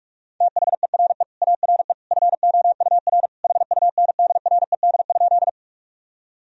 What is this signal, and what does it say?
Morse code, heard as a keyed tone, about 46 wpm, 700 Hz. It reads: THERE ARE FOUR HUNDRED?
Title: Sentences from Top 1000 Words - Review Part 2 - 45wpm